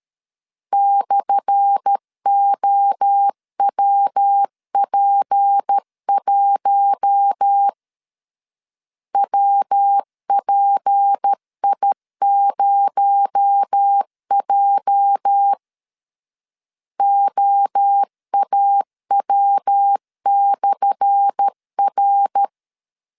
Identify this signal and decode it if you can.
Morse code, heard as a keyed tone, about 13 wpm, 800 Hz. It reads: /OWP1 WPI0J OAW/R
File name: LEZIONI / 10 SIGNS / 10A SLASH 12wpm 2314